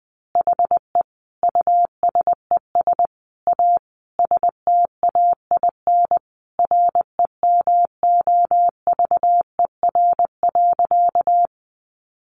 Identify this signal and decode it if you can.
Morse code, heard as a keyed tone, about 20 wpm, 700 Hz. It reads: HE USES A STAIN REMOVER.
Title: Additional Easy Phrases 20wpm